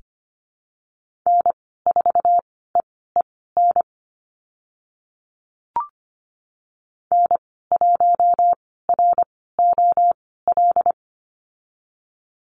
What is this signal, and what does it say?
Morse code, heard as a keyed tone, about 25 wpm, spaced as for 16 wpm, 690 Hz. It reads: N4EEN N1ROL
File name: Lesson 15g - Number 5 - Call Signs 25wpm